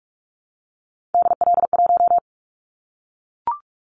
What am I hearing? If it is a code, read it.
Morse code, heard as a keyed tone, about 44 wpm, 700 Hz. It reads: DL1
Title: Callsign Prefixes for Top 25 DXCC - Repeated 3x - 45wpm